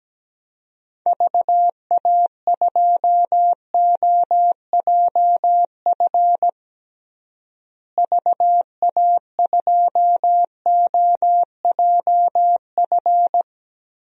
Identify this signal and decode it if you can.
Morse code, heard as a keyed tone, about 17 wpm, 700 Hz. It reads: VA2OJF VA2OJF
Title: DXCC-100 Call Signs - 6 Characters - Repeated 3x - 17wpm